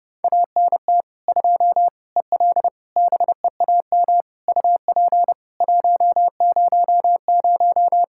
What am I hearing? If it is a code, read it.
Morse code, heard as a keyed tone, about 30 wpm, 700 Hz. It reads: ANT 2 EL BEAM UP 100
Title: Practice QSO - Exchange 2 - Info - 30wpm